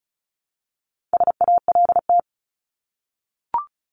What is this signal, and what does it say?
Morse code, heard as a keyed tone, about 34 wpm, 700 Hz. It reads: SALT
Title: Top 751-1000 Words 35wpm